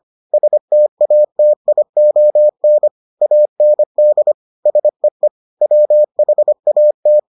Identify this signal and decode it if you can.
Morse code, heard as a keyed tone, about 25 wpm, 600 Hz. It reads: STATION AND SEE WHAT